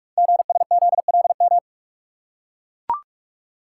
Morse code, heard as a keyed tone, about 44 wpm, 690 Hz. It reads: GSZLM
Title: Code Groups - 5 Letters - 45wpm